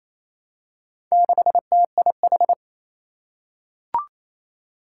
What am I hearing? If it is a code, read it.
Morse code, heard as a keyed tone, about 27 wpm, 700 Hz. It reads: 6TIH
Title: Four Challenging Letter-Number Combinations 28wpm